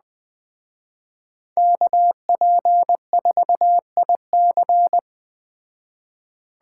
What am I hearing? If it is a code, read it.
Morse code, heard as a keyed tone, about 20 wpm, 700 Hz. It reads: KP4IC